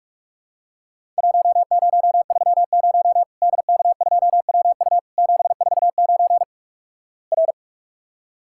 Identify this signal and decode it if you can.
Morse code, heard as a keyed tone, about 46 wpm, 700 Hz. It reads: 1030 DK2WU 749 R
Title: Worked All Europe DX Contest - QTC Exchange - 45wpm